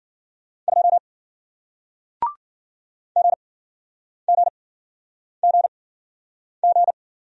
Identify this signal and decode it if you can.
Morse code, heard as a keyed tone, about 49 wpm, spaced as for 9 wpm, 700 Hz. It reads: 2 GGGG
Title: Speed-Racing Single Letter-Number 40wpm